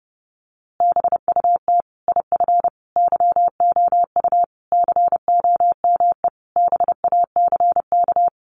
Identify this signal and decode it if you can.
Morse code, heard as a keyed tone, about 28 wpm, 700 Hz. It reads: BUT IF YOU COME BACK
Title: A Sound Of Thunder 30wpm